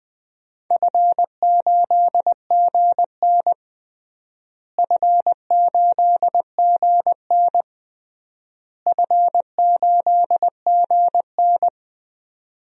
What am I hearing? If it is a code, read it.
Morse code, heard as a keyed tone, about 20 wpm, 700 Hz. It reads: F8GN F8GN F8GN
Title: DXCC-100 Call Signs - 4 Characters - Repeated 3x - 20wpm